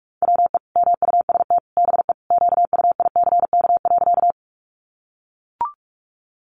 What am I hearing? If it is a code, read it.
Morse code, heard as a keyed tone, about 44 wpm, 700 Hz. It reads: WE MUST BE QUICK.
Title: Sentences from Top 500 Words 45wpm